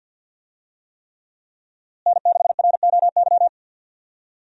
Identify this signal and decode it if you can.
Morse code, heard as a keyed tone, about 51 wpm, 700 Hz. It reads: N6ROY
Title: USA Call Signs - 5 Characters - Repeated 3x - 50wpm